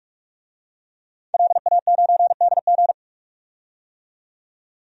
Morse code, heard as a keyed tone, about 46 wpm, 700 Hz. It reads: LA9DG
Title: DXCC-100 Call Signs - 5 Characters - 45wpm